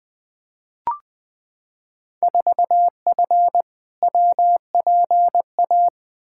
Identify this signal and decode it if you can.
Morse code, heard as a keyed tone, about 20 wpm, 700 Hz. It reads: 4F WPA
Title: Field Day Exchanges - 20wpm